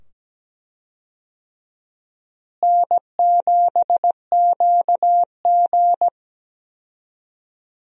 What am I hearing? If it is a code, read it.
Morse code, heard as a keyed tone, about 17 wpm, 700 Hz. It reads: N7QG
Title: USA Call Signs - 4 Characters - Repeated 2x - 17wpm